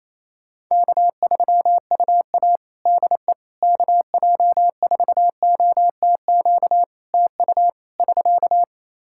Morse code, heard as a keyed tone, about 28 wpm, 700 Hz. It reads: K3UA DE KJ4OTQ TU <SK>
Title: Practice QSO - Exchange 4 - Ending - 28wpm